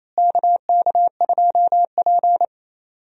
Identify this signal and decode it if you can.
Morse code, heard as a keyed tone, about 28 wpm, 700 Hz. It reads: KK2P
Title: USA Call Signs - 4 Characters - Repeated 3x - 28wpm